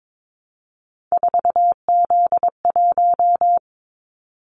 Morse code, heard as a keyed tone, about 22 wpm, 700 Hz. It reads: 4Z1
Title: Callsign Prefixes for Top 50 DXCC - 22wpm